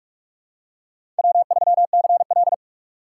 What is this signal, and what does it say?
Morse code, heard as a keyed tone, about 42 wpm, 700 Hz. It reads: W3CL
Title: USA Call Signs - 4 Characters - 6x Character Spacing - 45wpm